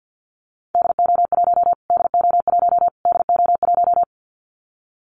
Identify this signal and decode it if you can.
Morse code, heard as a keyed tone, about 50 wpm, 700 Hz. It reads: DO1 DO1 DO1
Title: Callsign Prefixes for Top 25 DXCC - Repeated 3x - 50wpm